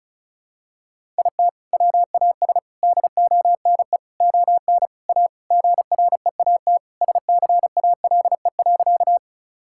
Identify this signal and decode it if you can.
Morse code, heard as a keyed tone, about 34 wpm, 700 Hz. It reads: IT WAS DONE ON A GREAT SCALE.
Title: Sentences from Top 1000 Words - Review Part 5 - 35wpm